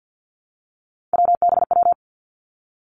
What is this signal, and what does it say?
Morse code, heard as a keyed tone, about 50 wpm, 700 Hz. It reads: WBW
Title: Three Challenging Letter-Number Combinations 50wpm